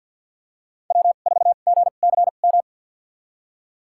Morse code, heard as a keyed tone, about 47 wpm, spaced as for 35 wpm, 700 Hz. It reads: W4GCM
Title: USA Call Signs - 5 Characters - 2x Character Spacing - 50wpm